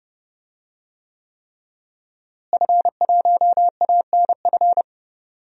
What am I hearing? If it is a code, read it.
Morse code, heard as a keyed tone, about 29 wpm, 700 Hz. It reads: F1ANF
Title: DXCC-100 Call Signs - 5 Characters - Repeated 3x - 30wpm